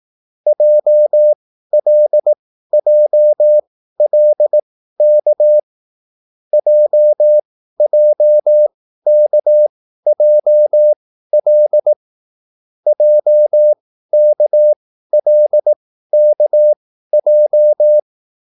Morse code, heard as a keyed tone, about 18 wpm, spaced as for 13 wpm, 600 Hz. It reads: JLJLK JJKJL JKLKJ